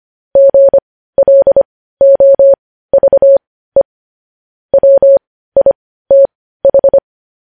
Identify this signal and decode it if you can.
Morse code, heard as a keyed tone, about 25 wpm, spaced as for 15 wpm, 550 Hz. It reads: GLOVE WITH
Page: News headlines in Morse code at 15 wpm